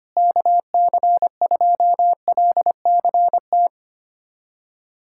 Morse code, heard as a keyed tone, about 25 wpm, 700 Hz. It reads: KC2LCT
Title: USA Callsign DE Callsign 25wpm